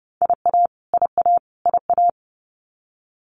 Morse code, heard as a keyed tone, about 30 wpm, 700 Hz. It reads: IA IA IA
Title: US States Repeated 3x 30wpm